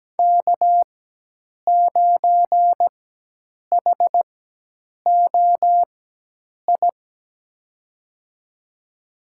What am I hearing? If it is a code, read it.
Morse code, heard as a keyed tone, about 17 wpm, spaced as for 8 wpm, 700 Hz. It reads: K9HOI
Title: USA Call Signs - 5 Characters - 4x Character Spacing - 17wpm